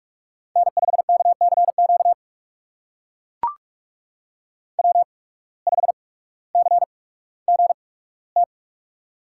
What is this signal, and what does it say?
Morse code, heard as a keyed tone, about 44 wpm, spaced as for 19 wpm, 700 Hz. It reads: N5KCQ W5CGT
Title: USA Call Signs - 5 Characters - 8x Character Spacing - 45wpm